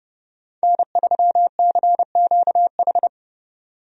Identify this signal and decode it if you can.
Morse code, heard as a keyed tone, about 30 wpm, 700 Hz. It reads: N3CQH
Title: USA Call Signs - 5 Characters - 30wpm